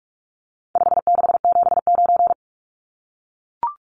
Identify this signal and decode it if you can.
Morse code, heard as a keyed tone, about 44 wpm, 700 Hz. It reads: 5679
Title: Numbers - 4-Digit 45wpm